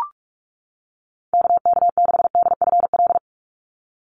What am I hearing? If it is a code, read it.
Morse code, heard as a keyed tone, about 44 wpm, 700 Hz. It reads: KK6DFL
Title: USA Call Signs 45wpm